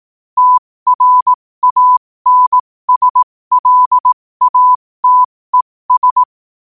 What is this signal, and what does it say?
Morse code, heard as a keyed tone, about 18 wpm, spaced as for 15 wpm, 1000 Hz. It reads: TRANSLATES
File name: qotdfiles / 15wpm